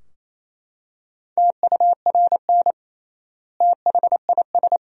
Morse code, heard as a keyed tone, about 28 wpm, 700 Hz. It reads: TURN THIS
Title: 3x Word-Spacing - Sentences from Top 200 Words - 28wpm